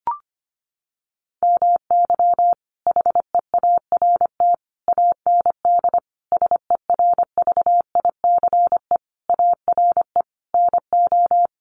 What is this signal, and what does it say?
Morse code, heard as a keyed tone, about 25 wpm, 700 Hz. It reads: MY HEART AND SERVICE ARE NO
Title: Sentences from Top 1200 Words 25wpm